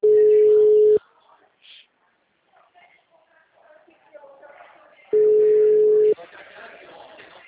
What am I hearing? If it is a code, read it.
Ringback tone (Europe, 425 Hz), with speech.